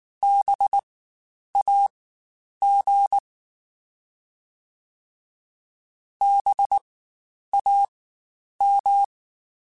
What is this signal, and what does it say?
Morse code, heard as a keyed tone, about 19 wpm, spaced as for 9 wpm, 800 Hz. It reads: BAG BAM
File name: Spaced Alphas / Spaced Alphas mp3s / Bs sp @18